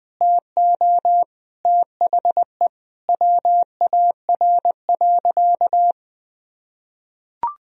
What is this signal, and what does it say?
Morse code, heard as a keyed tone, about 20 wpm, 700 Hz. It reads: TO THE WAR.
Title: Sentences from Top 400 Words and Abbreviations 20wpm